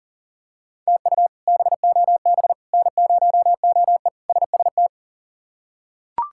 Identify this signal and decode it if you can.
Morse code, heard as a keyed tone, about 38 wpm, 700 Hz. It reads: TU BOB N0OE SST
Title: K1USN SST QSO Practice - 40wpm